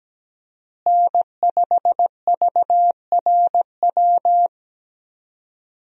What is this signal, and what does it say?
Morse code, heard as a keyed tone, about 17 wpm, 700 Hz. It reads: N5VRW
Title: USA Call Signs - 5 Characters - Repeated 3x - 17wpm